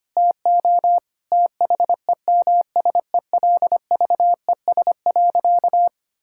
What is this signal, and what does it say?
Morse code, heard as a keyed tone, about 25 wpm, 700 Hz. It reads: TO THEMSELVES.
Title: Sentences from Top 1000 Words - Review Part 5 - 25wpm